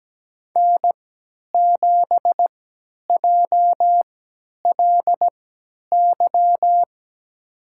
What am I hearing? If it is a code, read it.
Morse code, heard as a keyed tone, about 17 wpm, spaced as for 10 wpm, 700 Hz. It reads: N7JLY
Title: USA Call Signs - 3x Character Spacing - 17wpm